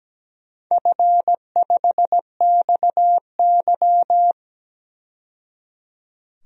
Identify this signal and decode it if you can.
Morse code, heard as a keyed tone, about 17 wpm, 700 Hz. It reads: F5XY
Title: DXCC-100 Call Signs - 4 Characters - 17wpm